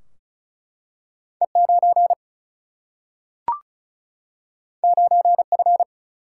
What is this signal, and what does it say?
Morse code, heard as a keyed tone, about 36 wpm, 700 Hz. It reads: E9 9F
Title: Single-Letter Single-Number 35wpm